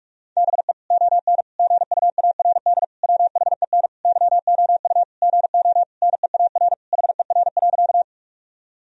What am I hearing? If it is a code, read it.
Morse code, heard as a keyed tone, about 46 wpm, 700 Hz. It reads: BE ON GUARD WHEN YOU GO NEAR HER.
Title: Sentences from Top 1000 Words - Review Part 4 - 45wpm